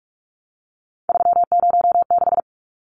Morse code, heard as a keyed tone, about 44 wpm, 700 Hz. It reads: 306
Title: Numbers - 3-Digit 45wpm